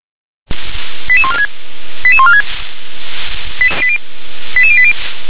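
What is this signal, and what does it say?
Five-tone selective call (ZVEI), heard as quick runs of five tones, tones of about 70 ms. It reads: 88146 88146 88080
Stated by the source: ZVEI five-tone selective call 88910, received with an SDR.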